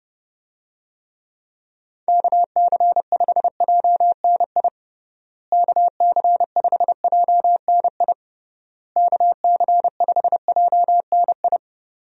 Morse code, heard as a keyed tone, about 30 wpm, 700 Hz. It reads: KC5JNI KC5JNI KC5JNI
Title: USA Call Signs - 6 Characters - Repeated 3x - 30wpm